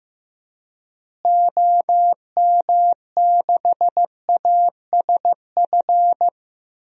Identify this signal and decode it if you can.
Morse code, heard as a keyed tone, about 15 wpm, 700 Hz. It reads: OM6ASF
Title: DXCC-100 Call Signs - 6 Characters - Repeated 3x - 15wpm